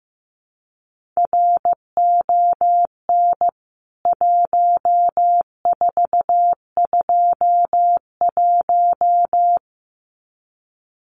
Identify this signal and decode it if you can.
Morse code, heard as a keyed tone, about 15 wpm, 700 Hz. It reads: RON 1421
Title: CWT Exchange 15wpm